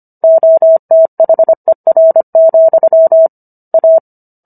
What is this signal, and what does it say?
Morse code, heard as a keyed tone, about 25 wpm, 650 Hz. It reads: OTHER, A